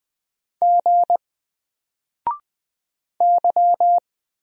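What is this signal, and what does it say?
Morse code, heard as a keyed tone, about 20 wpm, 700 Hz. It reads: G Y
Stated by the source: Single letter and number Morse drill at 20 wpm, from N4LCD.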